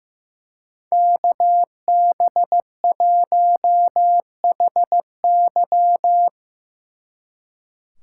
Morse code, heard as a keyed tone, about 17 wpm, 700 Hz. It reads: KB1HY
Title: Speed Racing - USA 2x2 Call Signs 15wpm